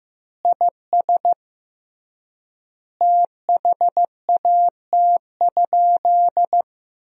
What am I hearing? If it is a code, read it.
Morse code, heard as a keyed tone, about 15 wpm, 700 Hz. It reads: IS THAT?
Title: 3x Word-Spacing - Sentences from Top 200 Words - 15wpm